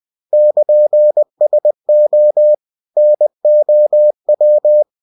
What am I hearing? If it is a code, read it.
Morse code, heard as a keyed tone, about 20 wpm, 600 Hz. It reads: <KN>SO NOW
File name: cw-books / winnie-the-pooh / winnie-20wpm-page-0104